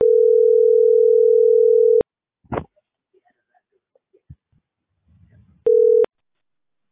Ringback tone (North America), 440 + 480 Hz.